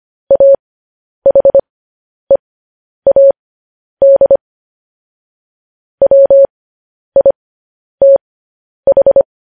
Morse code, heard as a keyed tone, about 25 wpm, spaced as for 10 wpm, 550 Hz. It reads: AHEAD WITH
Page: news headlines in morse code at 10 wpm